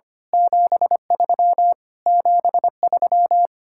Morse code, heard as a keyed tone, about 25 wpm, 700 Hz. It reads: 73 73
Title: Speed-Racing Common QSO Phrases Encore - 25wpm